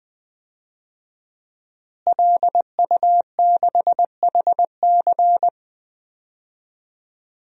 Morse code, heard as a keyed tone, about 20 wpm, 700 Hz. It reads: LU6HC
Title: DXCC-100 Call Signs - 5 Characters - Repeated 2x - 20wpm